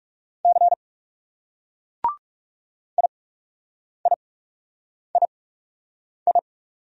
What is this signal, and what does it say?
Morse code, heard as a keyed tone, about 38 wpm, 700 Hz. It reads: C I I I I